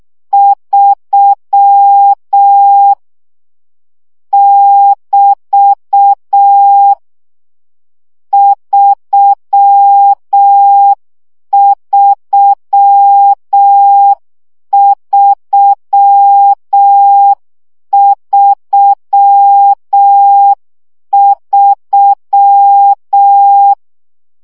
Morse code, heard as a keyed tone, about 6 wpm, 800 Hz. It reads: 3 <BT> 33333
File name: CW gyakorlas 3 5 baud